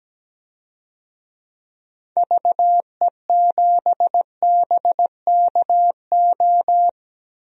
Morse code, heard as a keyed tone, about 17 wpm, 700 Hz. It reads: VE7BKO
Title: DXCC-100 Call Signs - 6 Characters - Repeated 3x - 17wpm